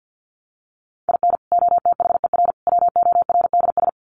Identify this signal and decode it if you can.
Morse code, heard as a keyed tone, about 50 wpm, 700 Hz. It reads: IN OTHER WORDS
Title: Sets of 3 Words 50wpm